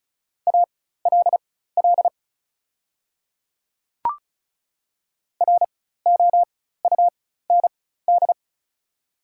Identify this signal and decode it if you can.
Morse code, heard as a keyed tone, about 36 wpm, spaced as for 16 wpm, 700 Hz. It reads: ALL ROUND
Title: 4x Character Spacing - Top 200 Words - 35wpm